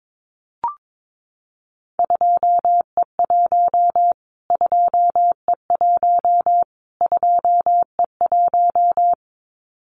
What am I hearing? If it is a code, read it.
Morse code, heard as a keyed tone, about 22 wpm, 700 Hz. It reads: 2E1 2E1 2E1
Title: Callsign Prefixes for Top 25 DXCC - Repeated 3x - 22wpm